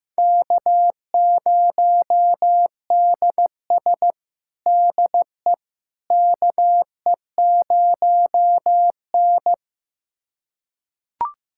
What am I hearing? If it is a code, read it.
Morse code, heard as a keyed tone, about 15 wpm, 700 Hz. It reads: K0DS DE KE0N
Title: USA Callsign DE Callsign - 4 Characters - 15wpm